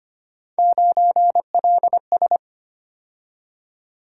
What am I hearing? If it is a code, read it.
Morse code, heard as a keyed tone, about 25 wpm, 700 Hz. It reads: 9LS
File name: Three Letter-Number Combinations - 25wpm